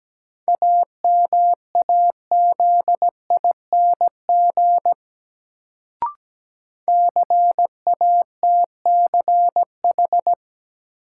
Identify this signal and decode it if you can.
Morse code, heard as a keyed tone, about 17 wpm, 700 Hz. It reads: AMAZING CATCH
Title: Sentences from Top 1500 Words - New Word Drill Alt - 17wpm